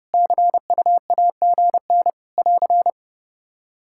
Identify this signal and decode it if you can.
Morse code, heard as a keyed tone, about 30 wpm, 700 Hz. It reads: CUAGN <AR>
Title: Practice QSO - Exchange 4 - Ending - 30wpm